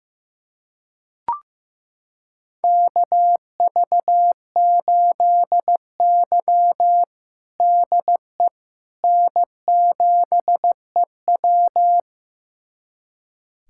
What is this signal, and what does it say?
Morse code, heard as a keyed tone, about 15 wpm, 700 Hz. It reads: KV8Y DE N7EW
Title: USA Callsign DE Callsign - 4 Characters - 15wpm